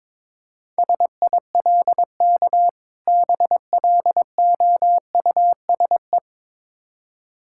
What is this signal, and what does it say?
Morse code, heard as a keyed tone, about 22 wpm, 690 Hz. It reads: SILK BLOUSE
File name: Difficult Words with Hints - 22wpm